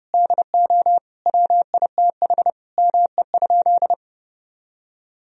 Morse code, heard as a keyed tone, about 30 wpm, 700 Hz. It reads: DO WITH ME?
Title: Sentences from Top 1000 Words - Review Part 2 - 30wpm